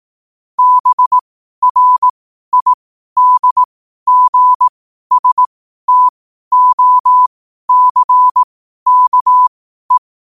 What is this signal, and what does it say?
Morse code, heard as a keyed tone, about 18 wpm, spaced as for 12 wpm, 1000 Hz. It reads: BRIDGSTOCKE